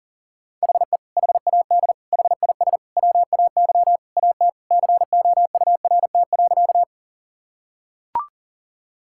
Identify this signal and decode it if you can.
Morse code, heard as a keyed tone, about 37 wpm, 700 Hz. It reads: HE HAD HIS WAY AT COURT.
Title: Sentences from Top 1600 Words 40wpm